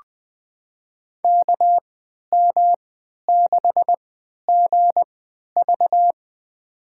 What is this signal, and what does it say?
Morse code, heard as a keyed tone, about 20 wpm, spaced as for 11 wpm, 700 Hz. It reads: KM6GV
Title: USA Call Signs - 5 Characters - 3x Character Spacing - 20wpm